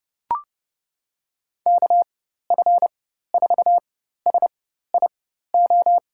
Morse code, heard as a keyed tone, about 31 wpm, spaced as for 14 wpm, 700 Hz. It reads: KF4SIO
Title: USA Call Signs - 6 Characters - 4x Character Spacing - 30wpm